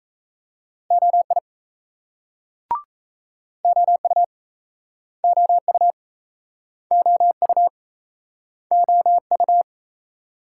Morse code, heard as a keyed tone, about 39 wpm, 700 Hz. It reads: OI OU OU OU OU